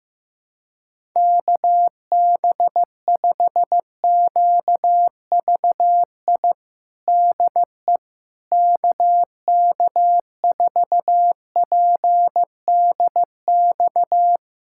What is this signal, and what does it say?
Morse code, heard as a keyed tone, about 15 wpm, 700 Hz. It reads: KB5QVI DE KK4PDX
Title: USA Callsign DE Callsign - 6 Characters - 15wpm